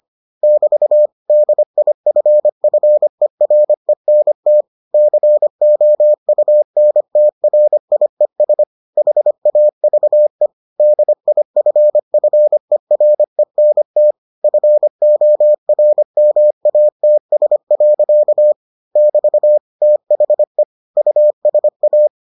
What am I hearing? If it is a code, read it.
Morse code, heard as a keyed tone, about 25 wpm, 600 Hz. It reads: <BT> DIFFERENT COUNTRIES HAVE DIFFERENT FORMATS. <BT> THE USA